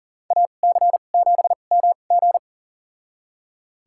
Morse code, heard as a keyed tone, about 40 wpm, spaced as for 29 wpm, 700 Hz. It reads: AC7MG
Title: USA Call Signs - 2x Character Spacing - 40wpm